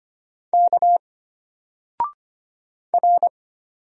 Morse code, heard as a keyed tone, about 25 wpm, 700 Hz. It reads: K R
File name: Single-Letter-Number-25wpm